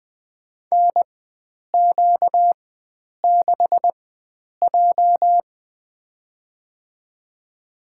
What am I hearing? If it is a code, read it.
Morse code, heard as a keyed tone, about 20 wpm, spaced as for 9 wpm, 700 Hz. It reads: NQ6J
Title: USA Call Signs - 4 Characters - 4x Character Spacing - 20wpm